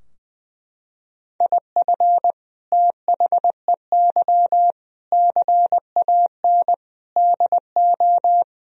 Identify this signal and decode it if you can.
Morse code, heard as a keyed tone, about 20 wpm, 700 Hz. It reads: IF THEY CAN DO
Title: Sentences from Top 1000 Words - Review Part 4 - 20wpm